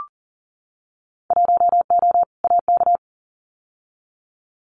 Morse code, heard as a keyed tone, about 38 wpm, 700 Hz. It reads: 1O AK
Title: Winter Field Day Exchanges Encore 40wpm